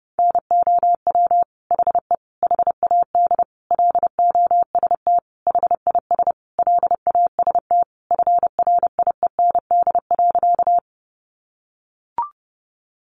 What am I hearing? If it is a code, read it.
Morse code, heard as a keyed tone, about 30 wpm, 700 Hz. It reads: NOW HE HAD LOST HIS LAST FRIEND.
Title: Sentences from Top 1400 Words 30wpm